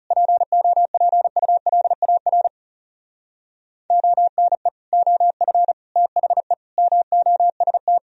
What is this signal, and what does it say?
Morse code, heard as a keyed tone, about 37 wpm, 700 Hz. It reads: POPULAR ONE OF THE MOST
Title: Speed-Racing - Sets of 5 Words 30wpm